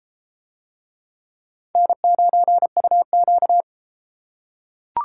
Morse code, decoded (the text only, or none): N9UQ